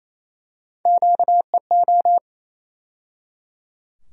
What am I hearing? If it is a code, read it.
Morse code, heard as a keyed tone, about 28 wpm, 700 Hz. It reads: QEO
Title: USA Call Sign Postfixes - 3 Letters 28wpm